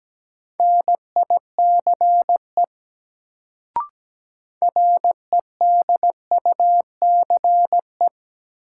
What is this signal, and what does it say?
Morse code, heard as a keyed tone, about 17 wpm, 700 Hz. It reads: NICE REDUCE
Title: Sentences from Top 1100 Words - New Word Drill - 17wpm